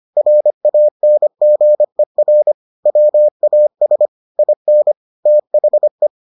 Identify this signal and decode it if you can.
Morse code, heard as a keyed tone, about 25 wpm, 600 Hz. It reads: RANGER WAS IN THE